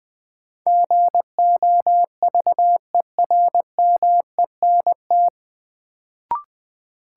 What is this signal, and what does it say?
Morse code, heard as a keyed tone, about 20 wpm, 700 Hz. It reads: GOVERMENT
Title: Sentences from Top 1300 Words - New Word Drill - 20wpm